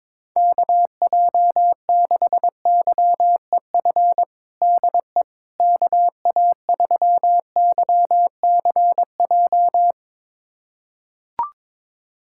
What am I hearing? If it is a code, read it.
Morse code, heard as a keyed tone, about 22 wpm, 700 Hz. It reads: KJ6YEF DE KA3YCJ
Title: USA Callsign DE Callsign 22wpm